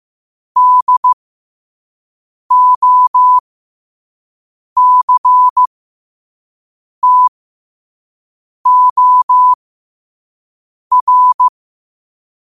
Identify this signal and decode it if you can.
Morse code, heard as a keyed tone, about 15 wpm, spaced as for 5 wpm, 1000 Hz. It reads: DOCTOR
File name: qotdfiles / 5wpm